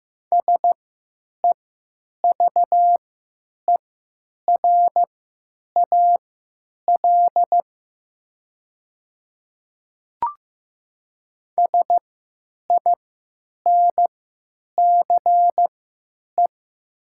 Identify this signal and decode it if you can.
Morse code, heard as a keyed tone, about 15 wpm, spaced as for 9 wpm, 700 Hz. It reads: SEVERAL SINCE
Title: 3x Character Spacing - Top 500 Words - 15wpm